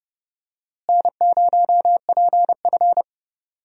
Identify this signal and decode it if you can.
Morse code, heard as a keyed tone, about 32 wpm, 700 Hz. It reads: N0PF
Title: USA Call Signs - 4 Characters - Repeated 2x - 30wpm